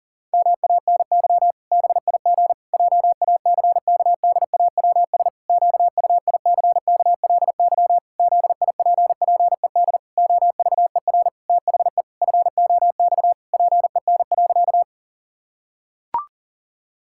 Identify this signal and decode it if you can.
Morse code, heard as a keyed tone, about 37 wpm, 700 Hz. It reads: MANY BIG JACKDAWS QUICKLY ZIPPED OVER THE FOX PEN.
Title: Pangram Challenge v2 - 40wpm